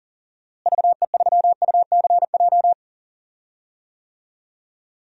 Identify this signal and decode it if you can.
Morse code, heard as a keyed tone, about 40 wpm, 700 Hz. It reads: VE3UCJ
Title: DXCC-100 Call Signs - 6 Characters - Repeated 3x - 40wpm